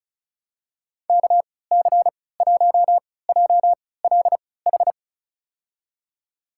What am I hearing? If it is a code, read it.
Morse code, heard as a keyed tone, about 36 wpm, spaced as for 20 wpm, 700 Hz. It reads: KC1JLH
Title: USA Call Signs - 6 Characters - 3x Character Spacing - 35wpm